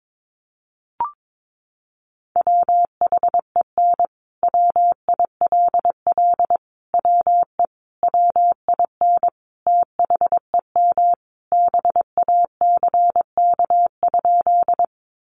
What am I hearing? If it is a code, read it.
Morse code, heard as a keyed tone, about 22 wpm, 700 Hz. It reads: WHEN WILL WE WIN THEM BACK?
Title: Sentences from Top 1500 Words 22wpm